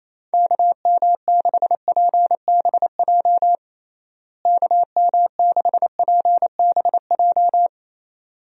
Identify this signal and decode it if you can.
Morse code, heard as a keyed tone, about 27 wpm, 700 Hz. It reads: KM6PBJ KM6PBJ
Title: USA Call Signs - 6 Characters - Repeated 2x - 28wpm